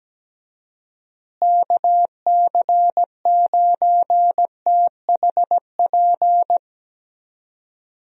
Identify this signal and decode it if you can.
Morse code, heard as a keyed tone, about 17 wpm, 700 Hz. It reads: KC9THP
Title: USA Call Signs - 6 Characters - Repeated 2x - 17wpm